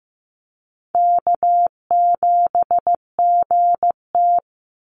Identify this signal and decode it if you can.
Morse code, heard as a keyed tone, about 15 wpm, 700 Hz. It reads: K7GT
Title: NA SOTA Chaser Callsigns 15wpm